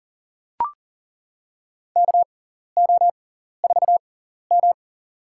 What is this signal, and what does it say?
Morse code, heard as a keyed tone, about 40 wpm, spaced as for 14 wpm, 700 Hz. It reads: KO4M
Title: USA Call Signs - 6x Character Spacing - 40wpm